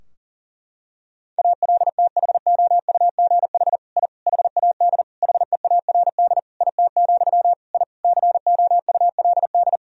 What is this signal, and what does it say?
Morse code, heard as a keyed tone, about 40 wpm, 700 Hz. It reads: ALTHOUGH I HAD HEARD IT, I COULD
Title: Sentences from Top 1000 Words - Review Part 3 - 40wpm